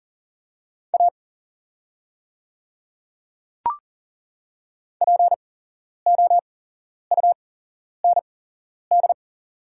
Morse code, heard as a keyed tone, about 38 wpm, spaced as for 11 wpm, 700 Hz. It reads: A POUND
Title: 8x Character Spacing - Top 500 Words - 40wpm